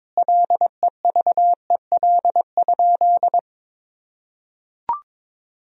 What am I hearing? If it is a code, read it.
Morse code, heard as a keyed tone, about 22 wpm, 700 Hz. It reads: LEVEL?
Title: Sentences from Top 1000 Words - Review Part 4 - 22wpm